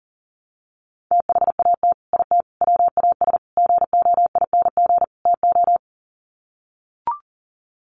Morse code, heard as a keyed tone, about 40 wpm, 700 Hz. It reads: THAT IT WAS GOING TO
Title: Sets of 5 Words Encore2 40wpm